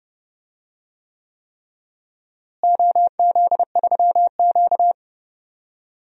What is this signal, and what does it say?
Morse code, heard as a keyed tone, about 30 wpm, 700 Hz. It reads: OZ3Q
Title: DXCC-100 Call Signs - 4 Characters - Repeated 3x - 30wpm